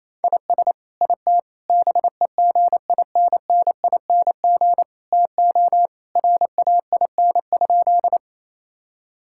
Morse code, heard as a keyed tone, about 27 wpm, 700 Hz. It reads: IS IT BEGINNING TO RAIN?
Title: Sentences from Top 1000 Words - Review Part 5 - 28wpm